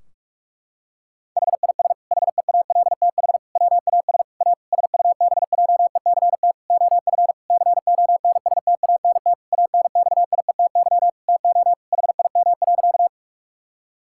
Morse code, heard as a keyed tone, about 46 wpm, 700 Hz. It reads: HIS HEALTH WAS A SUBJECT OF CONSTANT ANXIETY TO HIM.
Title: Sentences from Top 1600 Words 45wpm